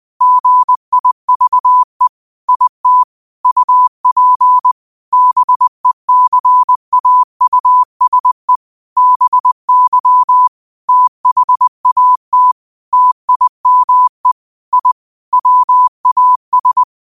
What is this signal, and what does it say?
Morse code, heard as a keyed tone, about 20 wpm, 1000 Hz. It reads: GIVE IT UP BECAUSE BY THAT TIME I WAS
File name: qotdfiles / 20wpm